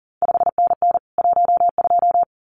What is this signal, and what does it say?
Morse code, heard as a keyed tone, about 39 wpm, 700 Hz. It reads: 5NN 12
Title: CQ WW DX Contest Exchange Encore - 40wpm